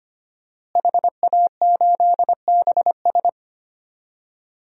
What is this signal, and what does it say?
Morse code, heard as a keyed tone, about 25 wpm, 700 Hz. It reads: HA8BS
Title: DXCC-100 Call Signs - 5 Characters - Repeated 2x - 25wpm